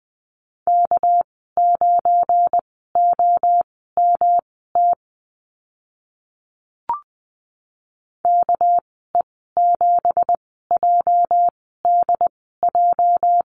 Morse code, heard as a keyed tone, about 20 wpm, spaced as for 14 wpm, 700 Hz. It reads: K9OMT KE7JDJ
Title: 238 Lesson 38g - Letter q - Call Signs 20wpm